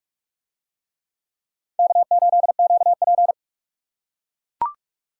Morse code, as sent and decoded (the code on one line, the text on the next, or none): -.- ---.. --.- .--.
K8QP